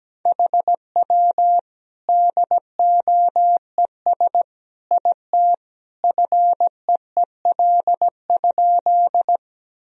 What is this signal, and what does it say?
Morse code, heard as a keyed tone, about 17 wpm, 700 Hz. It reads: HW DOES IT FEEL?
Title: Sentences from Top 400 Words and Abbreviations 17wpm